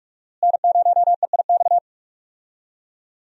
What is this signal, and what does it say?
Morse code, heard as a keyed tone, about 45 wpm, 700 Hz. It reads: N0EIX
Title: USA Call Signs - 5 Characters - Repeated 3x - 45wpm